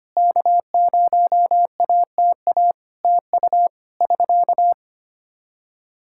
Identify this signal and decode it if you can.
Morse code, heard as a keyed tone, about 25 wpm, 700 Hz. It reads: K0ATA TU <SK>
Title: Practice QSO - Exchange 4 - Ending - 25wpm